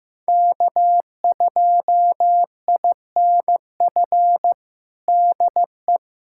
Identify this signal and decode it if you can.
Morse code, heard as a keyed tone, about 15 wpm, 700 Hz. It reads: K2INF DE
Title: USA Callsign DE Callsign - 5 Characters - 15wpm